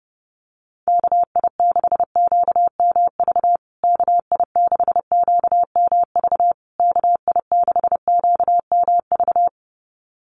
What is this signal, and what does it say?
Morse code, heard as a keyed tone, about 30 wpm, 700 Hz. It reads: KI6QMV KI6QMV KI6QMV
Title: USA Call Signs Repeated 3x 30wpm